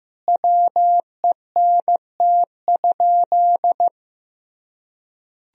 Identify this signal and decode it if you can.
Morse code, heard as a keyed tone, about 16 wpm, 700 Hz. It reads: WENT?
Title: Sentences from Top 1000 Words - Review Part 3 - 15wpm